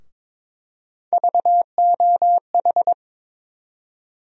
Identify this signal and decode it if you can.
Morse code, heard as a keyed tone, about 22 wpm, 700 Hz. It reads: VOH
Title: USA Call Sign Postfixes - 3 Letters 22wpm